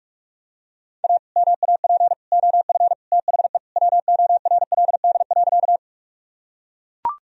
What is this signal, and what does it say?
Morse code, heard as a keyed tone, about 44 wpm, 700 Hz. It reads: A MAP OF THE WORLD.
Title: Sentences from Top 1000 Words - Review Part 2 - 45wpm